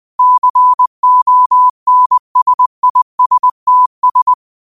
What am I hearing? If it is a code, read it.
Morse code, heard as a keyed tone, about 20 wpm, 1000 Hz. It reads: CONSISTS